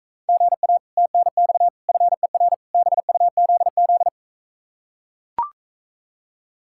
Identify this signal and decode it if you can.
Morse code, heard as a keyed tone, about 44 wpm, 700 Hz. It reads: GA TNX FER BUZZ